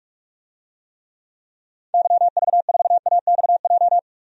Morse code, heard as a keyed tone, about 47 wpm, 700 Hz. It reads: YV4AXJ
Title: DXCC-100 Call Signs - 6 Characters - 45wpm